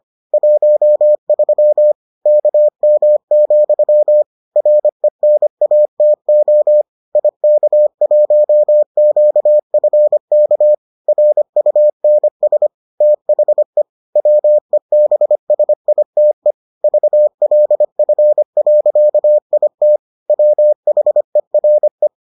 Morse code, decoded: 13 KM, RENATO IK1QFK RUNS THE WEBSITE VLF.IT WHERE

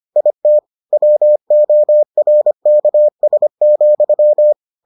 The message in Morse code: IT WORKS,

